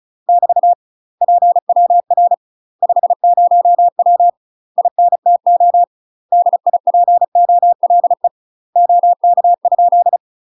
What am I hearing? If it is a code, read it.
Morse code, heard as a keyed tone, about 36 wpm, 700 Hz. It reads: <BT> PWR 50W INTO DIPOLE OK?
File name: UK MASTER QSOs TRAINING / 35WPM / UK MASTER QSOs 0036